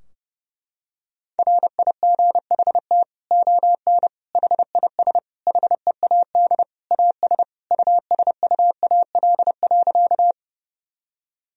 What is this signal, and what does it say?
Morse code, heard as a keyed tone, about 30 wpm, 700 Hz. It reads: RIGHT ON HIS HEAD AS USUAL.
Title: Sentences from Top 1000 Words - Review Part 2 - 30wpm